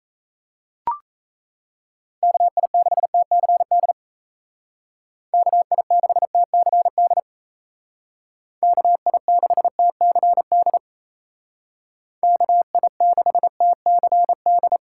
Morse code, heard as a keyed tone, about 33 wpm, 700 Hz. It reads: KI6TCD KI6TCD KI6TCD KI6TCD
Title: Speed Racing - USA Call Signs 28wpm